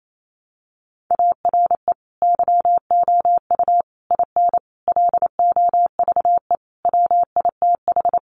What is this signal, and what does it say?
Morse code, heard as a keyed tone, about 28 wpm, 700 Hz. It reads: ARE YOU IN LOVE WITH
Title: Sentences from Top 400 Words 28wpm